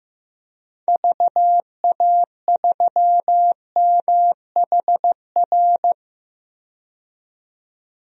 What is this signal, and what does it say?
Morse code, heard as a keyed tone, about 15 wpm, 700 Hz. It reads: VA3MHR
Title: DXCC-100 Call Signs - 6 Characters - 15wpm